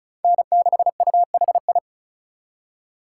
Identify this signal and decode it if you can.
Morse code, heard as a keyed tone, about 36 wpm, 700 Hz. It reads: N6UHI
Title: USA Call Signs - 5 Characters - 2x Character Spacing - 35wpm